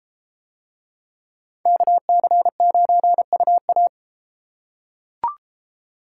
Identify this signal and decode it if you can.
Morse code, heard as a keyed tone, about 33 wpm, 700 Hz. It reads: KC9UA